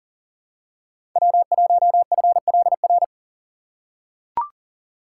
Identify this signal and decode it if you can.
Morse code, heard as a keyed tone, about 40 wpm, 700 Hz. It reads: W1FLR
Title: USA Call Signs - 5 Characters - 3x Character Spacing - 40wpm